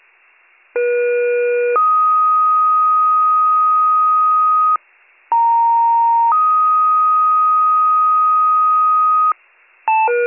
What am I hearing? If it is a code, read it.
Two-tone page, heard as a one-second tone, then a three-second tone, about 480 then 1250 Hz, about 930 then 1250 Hz. It is heard from 0.7 to 9.3 s.